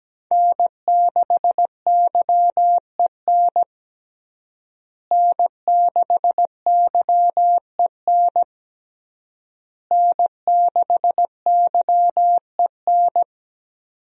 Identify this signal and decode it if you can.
Morse code, heard as a keyed tone, about 17 wpm, 700 Hz. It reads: N6YEN N6YEN N6YEN
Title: USA Call Signs - 5 Characters - Repeated 3x - 17wpm